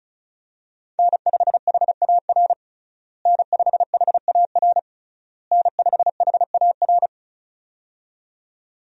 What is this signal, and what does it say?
Morse code, heard as a keyed tone, about 34 wpm, 700 Hz. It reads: N5HAR N5HAR N5HAR
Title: USA Call Signs - 5 Characters - Repeated 3x - 35wpm